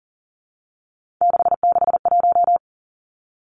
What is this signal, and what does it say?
Morse code, heard as a keyed tone, about 38 wpm, 700 Hz. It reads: 661
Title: Three Challenging Letter-Number Combinations 40wpm